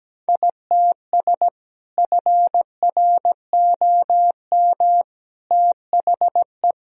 Morse code, heard as a keyed tone, about 17 wpm, 700 Hz. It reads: ITS FROM THE
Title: Sentences from Top 1000 Words - Review Part 2 - 17wpm